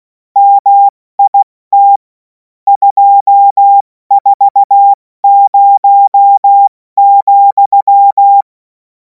Morse code, heard as a keyed tone, about 16 wpm, 800 Hz. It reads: MIT 240,
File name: Morsekurs / Witze / Witz 05. Mai 2026 70 F80